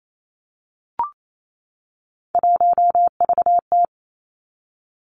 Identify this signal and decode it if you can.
Morse code, heard as a keyed tone, about 28 wpm, 700 Hz. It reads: 1VT
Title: Three Challenging Letter-Number Combinations 28wpm